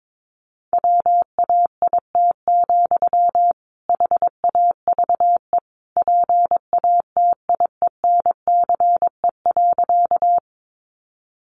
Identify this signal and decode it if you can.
Morse code, heard as a keyed tone, about 22 wpm, 700 Hz. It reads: WAIT, HAVE PATIENCE.
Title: Sentences from Top 1300 Words 22wpm